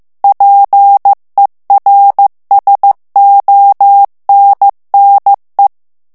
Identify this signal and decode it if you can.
Morse code, heard as a keyed tone, about 15 wpm, 800 Hz. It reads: PERSONNE
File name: FRANCAIS TEXTE CLAIR 2 VITESSE 14 MOTS MINUTES